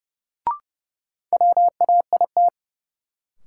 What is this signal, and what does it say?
Morse code, heard as a keyed tone, about 30 wpm, 700 Hz. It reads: WAIT
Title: Sentences from Top 500 Words - New Word Drill - 30wpm